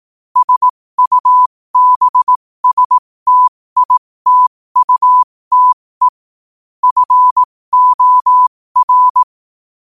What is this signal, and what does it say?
Morse code, heard as a keyed tone, about 18 wpm, spaced as for 15 wpm, 1000 Hz. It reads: SUBSTITUTE FOR